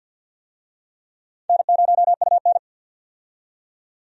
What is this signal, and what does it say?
Morse code, heard as a keyed tone, about 47 wpm, 700 Hz. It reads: N0UN
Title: USA Call Signs - 4 Characters - 50wpm